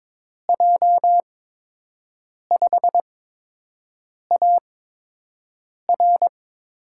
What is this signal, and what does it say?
Morse code, heard as a keyed tone, about 22 wpm, spaced as for 6 wpm, 700 Hz. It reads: J5AR
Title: USA Call Signs - 8x Character Spacing - 22wpm